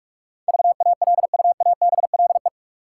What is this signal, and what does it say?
Morse code, heard as a keyed tone, about 47 wpm, 700 Hz. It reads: VALUABLE
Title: Sentences from Top 1100 Words - New Word Drill Alt - 45wpm